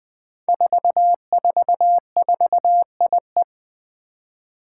Morse code, heard as a keyed tone, about 20 wpm, 700 Hz. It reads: 444IE
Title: Five Challenging Letter-Number Combinations 20wpm